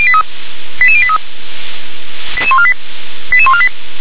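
Five-tone selective call (ZVEI), heard as quick runs of five tones, tones of about 70 ms. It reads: 88083 88148 88148
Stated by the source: ZVEI five-tone selective call received on SDR.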